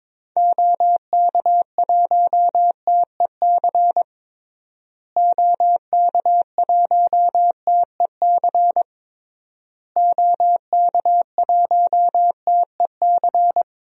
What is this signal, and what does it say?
Morse code, heard as a keyed tone, about 22 wpm, 700 Hz. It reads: OK1TEC OK1TEC OK1TEC
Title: DXCC-100 Call Signs - 6 Characters - Repeated 3x - 22wpm